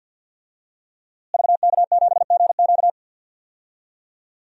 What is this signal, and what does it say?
Morse code, heard as a keyed tone, about 46 wpm, 700 Hz. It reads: VK7GQ
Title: DXCC-100 Call Signs - 5 Characters - Repeated 2x - 50wpm